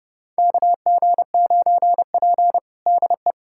Morse code, heard as a keyed tone, about 32 wpm, 700 Hz. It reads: KG9P DE
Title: USA Callsign DE Callsign - 4 Characters - 30wpm